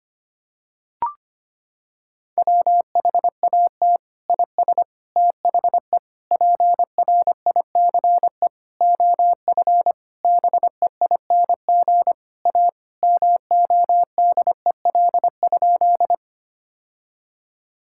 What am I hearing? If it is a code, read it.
Morse code, heard as a keyed tone, about 25 wpm, 700 Hz. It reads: WHAT IS THE PRICE OF BEING A MODEL?
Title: Sentences from Top 1000 Words - Review Part 5 - 25wpm